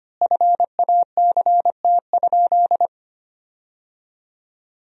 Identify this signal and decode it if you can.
Morse code, heard as a keyed tone, about 25 wpm, 700 Hz. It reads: FACT?
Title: 3x Word-Spacing - Sentences from Top 500 Words - 25wpm